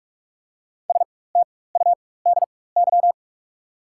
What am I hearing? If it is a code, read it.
Morse code, heard as a keyed tone, about 45 wpm, spaced as for 21 wpm, 700 Hz. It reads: STUDY
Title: 4x Character Spacing - Top 500 Words - 45wpm